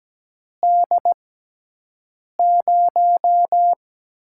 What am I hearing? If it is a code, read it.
Morse code, heard as a keyed tone, about 17 wpm, spaced as for 6 wpm, 700 Hz. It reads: D0
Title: USA Call Signs - 4 Characters - 6x Character Spacing - 17wpm